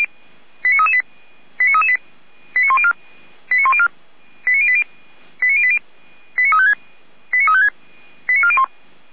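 Five-tone selective call (ZVEI), heard as quick runs of five tones, tones of about 70 ms. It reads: 89338 89338 89104 89104 89080 89080 89356 89356 89401
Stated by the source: ZVEI five-tone selective call 89126 received by SDR.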